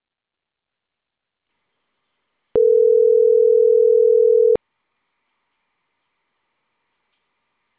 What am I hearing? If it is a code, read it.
Ringback tone (North America), 440 + 480 Hz.